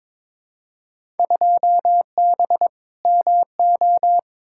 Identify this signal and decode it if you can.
Morse code, heard as a keyed tone, about 22 wpm, 700 Hz. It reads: 2B MO
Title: Field Day Exchanges - 22wpm